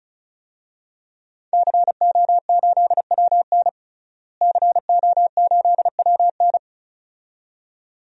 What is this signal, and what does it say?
Morse code, heard as a keyed tone, about 34 wpm, 700 Hz. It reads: CO8WN CO8WN
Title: DXCC-100 Call Signs - 5 Characters - Repeated 2x - 35wpm